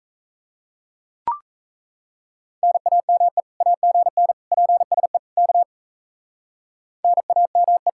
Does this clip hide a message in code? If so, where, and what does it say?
Yes: Morse code from 2.6 to 7.9 s, heard as a keyed tone, about 35 wpm, 700 Hz. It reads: NAME AGN PSE K NAME